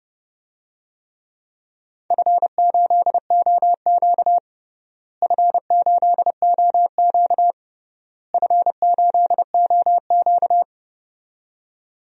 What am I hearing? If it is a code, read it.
Morse code, heard as a keyed tone, about 32 wpm, 700 Hz. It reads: F8OQ F8OQ F8OQ
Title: DXCC-100 Call Signs - 4 Characters - Repeated 3x - 30wpm